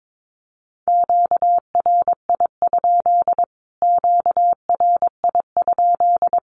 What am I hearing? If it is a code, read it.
Morse code, heard as a keyed tone, about 22 wpm, 700 Hz. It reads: QRI? QRI?
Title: Q-Codes Repeated 3x 22wpm